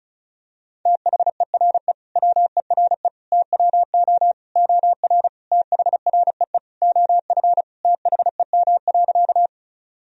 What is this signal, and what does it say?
Morse code, heard as a keyed tone, about 34 wpm, 700 Hz. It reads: THERE WERE TWO OR THREE OF THEM.
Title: Sentences from Top 1000 Words - Review Part 1 - 35wpm